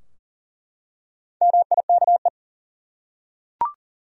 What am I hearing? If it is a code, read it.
Morse code, heard as a keyed tone, about 40 wpm, 700 Hz. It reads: MIKE
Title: USA Common Names - Max 4 Letters - 40wpm